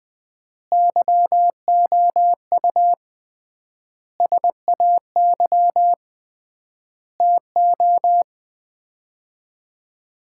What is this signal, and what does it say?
Morse code, heard as a keyed tone, about 21 wpm, 700 Hz. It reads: YOU SAY TO
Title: 3x Word-Spacing - Sets of 5 Words - 20wpm